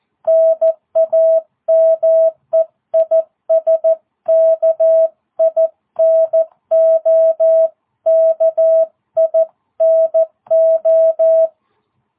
Morse code, heard as a keyed tone, about 14 wpm, 660 Hz. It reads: NAME IS KINO KINO